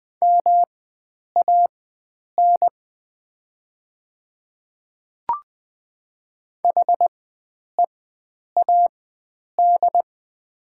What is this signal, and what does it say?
Morse code, heard as a keyed tone, about 20 wpm, spaced as for 9 wpm, 700 Hz. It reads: MAN HEAD
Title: 4x Character Spacing - Top 300 Words - 20wpm